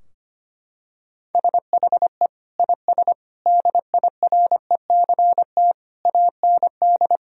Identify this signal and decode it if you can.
Morse code, heard as a keyed tone, about 25 wpm, 700 Hz. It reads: SHE IS DIRECT AND
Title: Sentences from Top 1000 Words - Review Part 4 - 25wpm